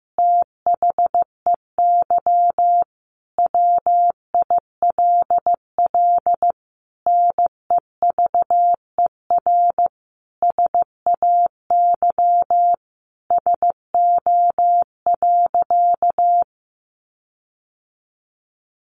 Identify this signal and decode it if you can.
Morse code, heard as a keyed tone, about 15 wpm, 700 Hz. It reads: THEY WILL NEVER SAY SO.
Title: Sentences from Top 1400 Words 15wpm